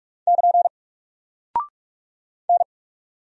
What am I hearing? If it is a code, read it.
Morse code, heard as a keyed tone, about 44 wpm, 700 Hz. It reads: <KN> N
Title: Single Character (Letter:Number:Punctuation:Prosign) Alt 45wpm